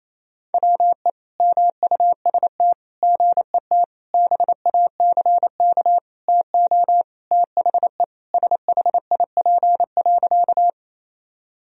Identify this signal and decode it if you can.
Morse code, heard as a keyed tone, about 28 wpm, 700 Hz. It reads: WE MUST GET BACK TO THE SHIP.
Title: Sentences from Top 1000 Words - Review Part 2 - 28wpm